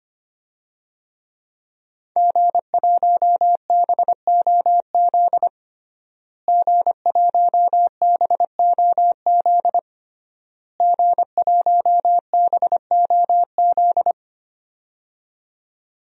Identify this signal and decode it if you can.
Morse code, heard as a keyed tone, about 25 wpm, 700 Hz. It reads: G1BOZ G1BOZ G1BOZ
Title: DXCC-100 Call Signs - 5 Characters - Repeated 3x - 25wpm